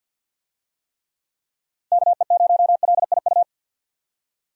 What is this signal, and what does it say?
Morse code, heard as a keyed tone, about 46 wpm, 700 Hz. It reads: KE0LIU
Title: USA Call Signs - 6 Characters - 50wpm